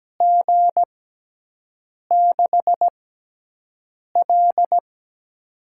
Morse code, heard as a keyed tone, about 17 wpm, spaced as for 6 wpm, 700 Hz. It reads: G6L
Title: USA Call Signs - 4 Characters - 6x Character Spacing - 17wpm